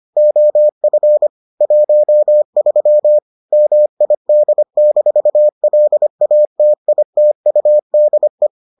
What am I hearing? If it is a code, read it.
Morse code, heard as a keyed tone, about 25 wpm, 600 Hz. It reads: OF 13 MID-LATITUDE